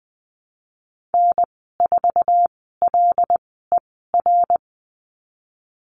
Morse code, heard as a keyed tone, about 20 wpm, spaced as for 15 wpm, 700 Hz. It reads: N4LER
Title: 082 Lesson 14g - Number 2 - Call Signs 20wpm